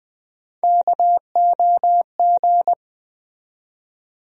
Morse code, heard as a keyed tone, about 20 wpm, 700 Hz. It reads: KOG
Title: USA Call Sign Postfixes - 3 Letters 20wpm